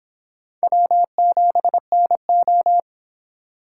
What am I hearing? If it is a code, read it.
Morse code, heard as a keyed tone, about 24 wpm, 700 Hz. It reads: W7NO